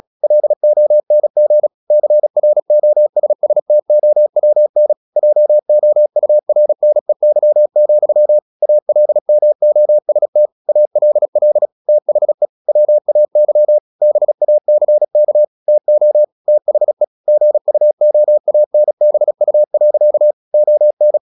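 Morse code, heard as a keyed tone, about 36 wpm, 600 Hz. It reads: LONG CROSSTOWN JOURNEY, ALMOST ALL THE WAY BACK TO THE GUOANBU. ON
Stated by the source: Morse practice at 36 wpm, text of Ken Follett's novel Never, from CW med Gnister.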